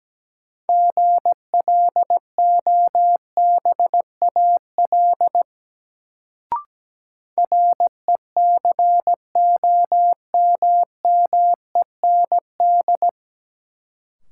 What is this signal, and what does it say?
Morse code, heard as a keyed tone, about 17 wpm, 700 Hz. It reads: GLOBAL RECOMMEND
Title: Sentences from Top 1200 Words - New Word Drill - 17wpm